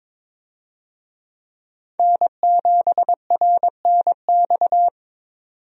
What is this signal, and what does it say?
Morse code, heard as a keyed tone, about 22 wpm, 700 Hz. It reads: N7RNX
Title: USA Call Signs - 5 Characters - 22wpm